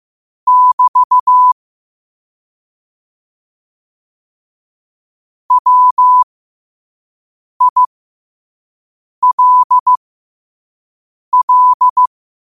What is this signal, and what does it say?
Morse code, heard as a keyed tone, about 15 wpm, spaced as for 5 wpm, 1000 Hz. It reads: <BT> WILL
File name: qotdfiles / 5wpm